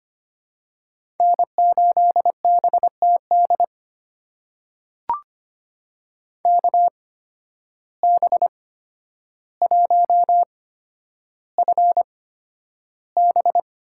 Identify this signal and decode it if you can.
Morse code, heard as a keyed tone, about 25 wpm, spaced as for 11 wpm, 700 Hz. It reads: N8BTD KB1FB
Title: USA Call Signs - 5 Characters - 8x Character Spacing - 25wpm